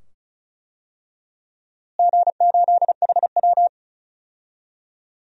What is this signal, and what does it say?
Morse code, heard as a keyed tone, about 36 wpm, 700 Hz. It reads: G8HW
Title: DXCC-100 Call Signs - 4 Characters - Repeated 2x - 35wpm